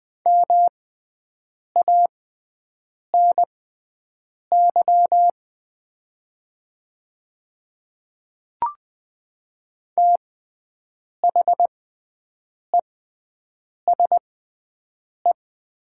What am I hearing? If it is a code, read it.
Morse code, heard as a keyed tone, about 20 wpm, spaced as for 7 wpm, 690 Hz. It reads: MANY THESE
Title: 6x Character Spacing - Top 100 Words - 20wpm